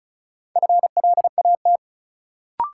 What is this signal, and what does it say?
Morse code, heard as a keyed tone, about 36 wpm, 700 Hz. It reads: FLAT
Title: Sentences from Top 1000 Words - New Word Drill - 35wpm